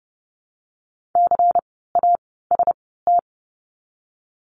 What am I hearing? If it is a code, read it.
Morse code, heard as a keyed tone, about 30 wpm, 700 Hz. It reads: C A S T
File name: Lesson 16f - Letter c - Words 30wpm